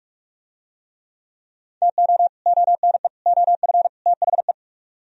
Morse code, heard as a keyed tone, about 45 wpm, 700 Hz. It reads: TO ONE OF THE